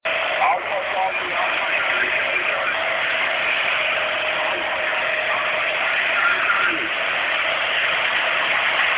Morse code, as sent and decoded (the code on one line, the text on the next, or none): none